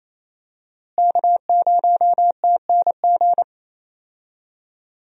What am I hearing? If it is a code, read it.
Morse code, heard as a keyed tone, about 27 wpm, 700 Hz. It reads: K0TNG
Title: USA Call Signs - 5 Characters - Repeated 2x - 28wpm